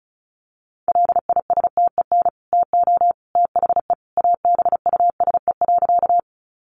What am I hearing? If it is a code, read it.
Morse code, heard as a keyed tone, about 34 wpm, 700 Hz. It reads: LISTEN TO THE ABUSE.
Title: Sentences from Top 1300 Words 35wpm